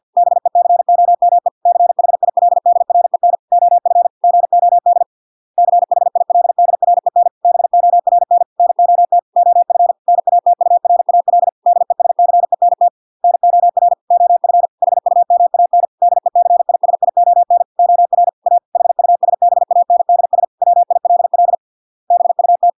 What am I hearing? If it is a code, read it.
Morse code, heard as a keyed tone, about 50 wpm, 700 Hz. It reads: BECOME CHILDREN OF GOD CHILDREN BORN NOT OF NATURAL DESCENT NOR OF HUMAN DECISION OR A HUSBANDS WILL BUT